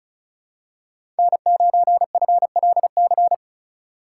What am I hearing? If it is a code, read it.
Morse code, heard as a keyed tone, about 36 wpm, 700 Hz. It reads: N9FLC
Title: USA Call Signs - 5 Characters - 3x Character Spacing - 35wpm